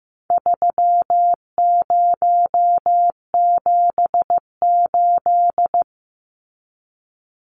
Morse code, heard as a keyed tone, about 15 wpm, 700 Hz. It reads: 3078
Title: Numbers - 4-Digit 15wpm